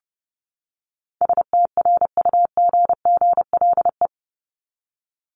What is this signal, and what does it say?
Morse code, heard as a keyed tone, about 28 wpm, 700 Hz. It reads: STRUGGLE
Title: Top 1751-2000 Words 30wpm